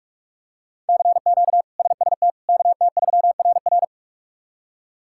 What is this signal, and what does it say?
Morse code, heard as a keyed tone, about 46 wpm, 700 Hz. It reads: CQ SST KT3RR
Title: K1USN SST QSO Practice - 45wpm